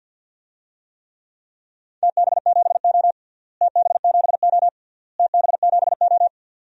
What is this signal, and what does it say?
Morse code, heard as a keyed tone, about 52 wpm, 700 Hz. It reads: TB7O TB7O TB7O
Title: DXCC-100 Call Signs - 4 Characters - Repeated 3x - 50wpm